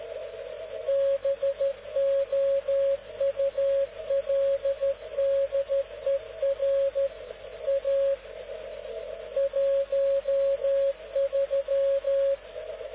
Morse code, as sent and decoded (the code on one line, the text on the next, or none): -... --- ..- .-.. -.. .-. / .- / .---- ...--
BOULDER A 13